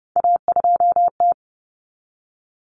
Morse code, heard as a keyed tone, about 30 wpm, 700 Hz. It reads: A2T
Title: Common Cut Numbers - 3-Digit 30wpm